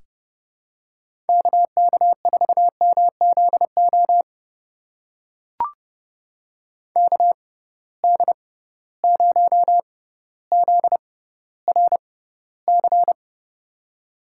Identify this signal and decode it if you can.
Morse code, heard as a keyed tone, about 31 wpm, spaced as for 15 wpm, 700 Hz. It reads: KK4MZO KD0ZRC
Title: USA Call Signs - 6 Characters - 6x Character Spacing - 30wpm